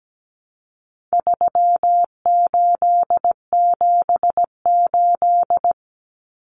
Morse code, heard as a keyed tone, about 17 wpm, 700 Hz. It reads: 3878
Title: Numbers - 4-Digit 17wpm